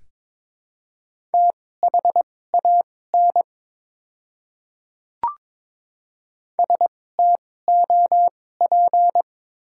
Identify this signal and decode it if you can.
Morse code, heard as a keyed tone, about 21 wpm, spaced as for 16 wpm, 700 Hz. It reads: THAN STOP
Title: 2x Character Spacing - Top 300 Words - 22wpm